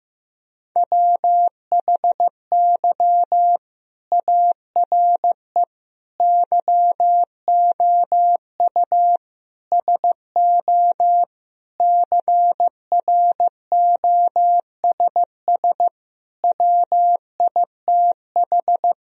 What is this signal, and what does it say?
Morse code, heard as a keyed tone, about 15 wpm, 700 Hz. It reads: WHY ARE YOU SO CROSS WITH
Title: Sentences from Top 1000 Words - Review Part 1 - 15wpm